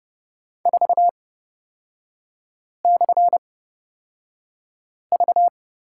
Morse code, heard as a keyed tone, about 30 wpm, spaced as for 5 wpm, 700 Hz. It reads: 4/V
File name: Lesson 36e - BT - Three Characters 30wpm